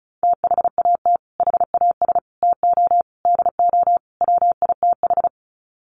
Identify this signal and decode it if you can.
Morse code, heard as a keyed tone, about 34 wpm, 700 Hz. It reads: THAT HAS TO DO WITH
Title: Sets of 5 Words Encore2 35wpm